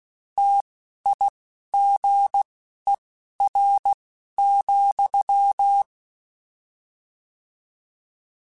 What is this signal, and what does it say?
Morse code, heard as a keyed tone, about 16 wpm, spaced as for 11 wpm, 800 Hz. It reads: TIGER,
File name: Misc / Misc-mp3s / Word Pairs @10